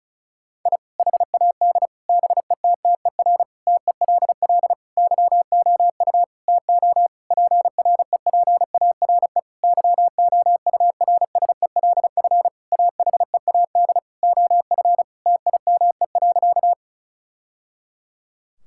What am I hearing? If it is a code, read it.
Morse code, heard as a keyed tone, about 34 wpm, 700 Hz. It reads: I HAD BETTER TELL YOU TO PREPARE YOURSELF AHEAD OF TIME.
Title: Sentences from Top 1000 Words - Review Part 6 - 35wpm